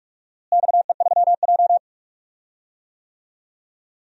Morse code, heard as a keyed tone, about 45 wpm, 700 Hz. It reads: XE3J
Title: DXCC-100 Call Signs - 4 Characters - Repeated 2x - 45wpm